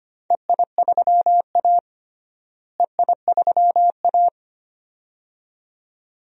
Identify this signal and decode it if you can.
Morse code, heard as a keyed tone, about 25 wpm, 700 Hz. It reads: EI3A EI3A
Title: DXCC-100 Call Signs - 4 Characters - Repeated 2x - 25wpm